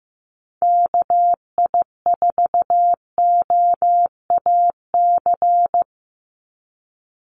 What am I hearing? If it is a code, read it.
Morse code, heard as a keyed tone, about 15 wpm, 700 Hz. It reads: KI4OAC
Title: USA Call Signs 15wpm